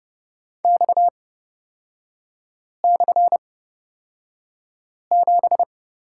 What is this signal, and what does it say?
Morse code, heard as a keyed tone, about 29 wpm, spaced as for 5 wpm, 700 Hz. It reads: X/7
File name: Lesson 37e - Letter x - Three Characters 30wpm